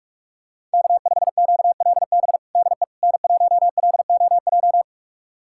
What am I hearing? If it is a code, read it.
Morse code, heard as a keyed tone, about 42 wpm, 700 Hz. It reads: K5QLB DE N1LOJ
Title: USA Callsign DE Callsign - 5 Characters - 45wpm